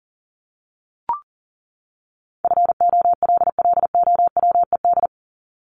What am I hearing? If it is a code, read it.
Morse code, heard as a keyed tone, about 40 wpm, 700 Hz. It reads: FOLLOWED
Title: ED Suffix - Top 100 Words with Base Words 40wpm